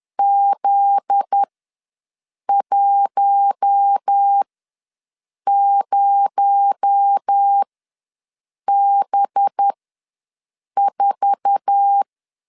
Morse code, heard as a keyed tone, about 10 wpm, spaced as for 6 wpm, 800 Hz. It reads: Z10B4
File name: LEZIONI / 9 2867 ws 10wpm 4215